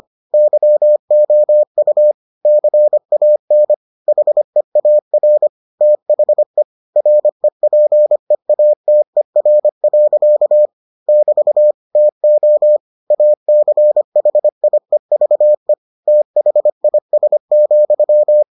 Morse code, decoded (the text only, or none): YOU CAN HEAR THE REPEATER. <BT> TO ACHIEVE THIS,